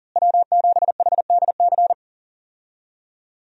Morse code, heard as a keyed tone, about 40 wpm, 700 Hz. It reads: W7HDC
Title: USA Call Signs - 5 Characters - Repeated 2x - 40wpm